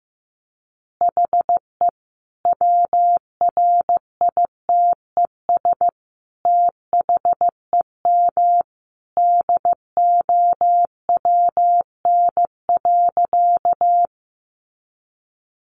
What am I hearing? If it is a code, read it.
Morse code, heard as a keyed tone, about 15 wpm, 700 Hz. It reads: HE WRITES THEM DOWN.
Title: Additional Easy Phrases 15wpm